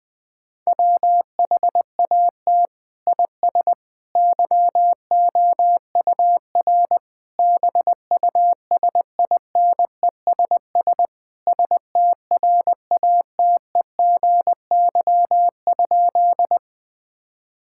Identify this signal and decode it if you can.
Morse code, heard as a keyed tone, about 20 wpm, 700 Hz. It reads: WHAT IS YOUR BUSINESS STRATEGY?
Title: Sentences from Top 1000 Words - Review Part 5 - 20wpm